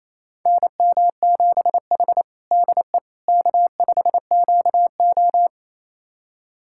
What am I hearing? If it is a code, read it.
Morse code, heard as a keyed tone, about 28 wpm, 700 Hz. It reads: NM7H DE K5QO
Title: USA Callsign DE Callsign - 4 Characters - 28wpm